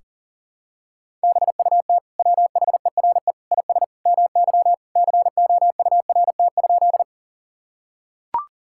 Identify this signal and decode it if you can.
Morse code, heard as a keyed tone, about 40 wpm, 700 Hz. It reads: BUT WHERE IS MY COURT?
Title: Sentences from Top 1600 Words 40wpm